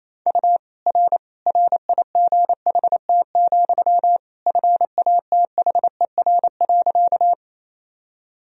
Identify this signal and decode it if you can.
Morse code, heard as a keyed tone, about 28 wpm, 700 Hz. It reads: U R RIGHT, FATHER.
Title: Sentences from Top 300 Words and Abbreviations 28wpm